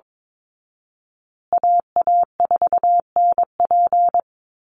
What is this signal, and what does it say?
Morse code, heard as a keyed tone, about 22 wpm, 700 Hz. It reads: AA4NP
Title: Call Signs Repeated 2x - 22wpm